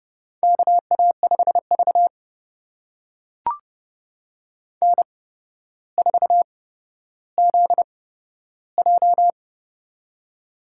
Morse code, heard as a keyed tone, about 31 wpm, spaced as for 13 wpm, 700 Hz. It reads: KA5V N4ZJ
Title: USA Call Signs - 4 Characters - 8x Character Spacing - 30wpm